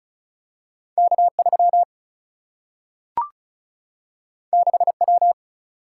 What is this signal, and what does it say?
Morse code, heard as a keyed tone, about 36 wpm, 700 Hz. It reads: K 3 6 W
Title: Single-Letter Single-Number 35wpm